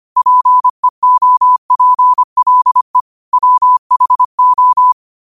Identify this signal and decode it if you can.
Morse code, heard as a keyed tone, about 25 wpm, 1000 Hz. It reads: PEOPLE WHO
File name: qotdfiles / 25wpm